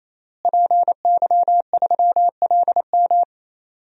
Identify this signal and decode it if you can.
Morse code, heard as a keyed tone, about 28 wpm, 700 Hz. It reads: PY3LM
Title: DXCC-100 Call Signs - 5 Characters - Repeated 3x - 28wpm